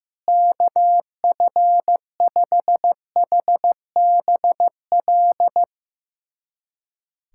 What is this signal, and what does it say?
Morse code, heard as a keyed tone, about 15 wpm, 700 Hz. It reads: KF5HBL
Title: USA Callsign DE Callsign - 6 Characters - 15wpm